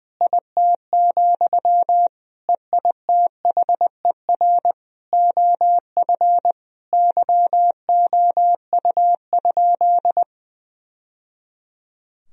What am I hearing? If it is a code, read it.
Morse code, heard as a keyed tone, about 21 wpm, 700 Hz. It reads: IT, EITHER OF YOU?
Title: Sentences from Top 1000 Words - Review Part 4 - 20wpm